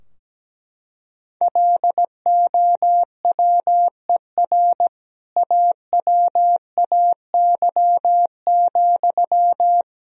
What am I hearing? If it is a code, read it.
Morse code, heard as a keyed tone, about 17 wpm, 700 Hz. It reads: LOWER AWAY,
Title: Sentences from Top 1000 Words - Review Part 4 - 17wpm